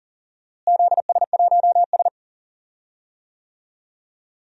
Morse code, heard as a keyed tone, about 40 wpm, 700 Hz. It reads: ZS1S
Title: DXCC-100 Call Signs - 4 Characters - Repeated 2x - 40wpm